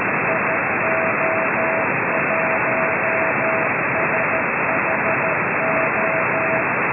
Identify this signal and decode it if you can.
Morse code, heard as a keyed tone, about 13 wpm, 650 Hz, with noise. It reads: IO1SHO